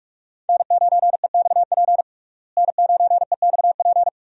Morse code, heard as a keyed tone, about 46 wpm, 700 Hz. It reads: N9EXP N9EXP
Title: USA Call Signs - 5 Characters - Repeated 3x - 45wpm